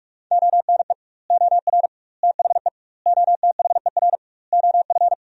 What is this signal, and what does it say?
Morse code, heard as a keyed tone, about 43 wpm, 700 Hz. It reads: ONE OR THE OTHER OF
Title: 2x Word-Spacing - Sentences from Top 400 Words - 45wpm